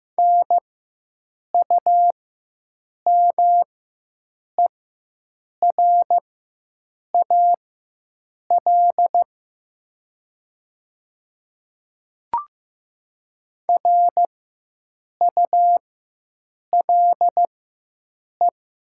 Morse code, heard as a keyed tone, about 15 wpm, spaced as for 7 wpm, 700 Hz. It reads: NUMERAL RULE
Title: 4x Character Spacing - Top 500 Words - 15wpm